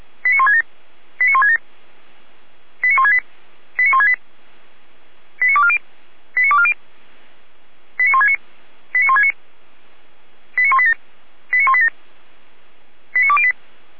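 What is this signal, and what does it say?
Five-tone selective call (ZVEI), heard as quick runs of five tones, tones of about 70 ms. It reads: 89167 89167 89168 89168 89240 89240 89169 89169 89187 89187 89208
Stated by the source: ZVEI five-tone selective call 89163 received on SDR.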